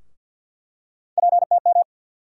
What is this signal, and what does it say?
Morse code, heard as a keyed tone, about 47 wpm, 700 Hz. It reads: PTM
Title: USA Call Sign Postfixes - 3 Letters 50wpm